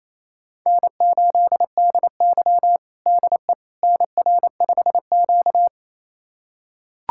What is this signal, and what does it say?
Morse code, heard as a keyed tone, about 28 wpm, 700 Hz. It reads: N8DY DE NR5Q
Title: USA Callsign DE Callsign - 4 Characters - 28wpm